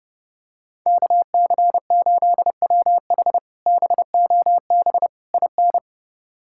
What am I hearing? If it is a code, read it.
Morse code, heard as a keyed tone, about 29 wpm, 700 Hz. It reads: KC8WH BOB IN
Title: K1USN SST QSO Practice - 30wpm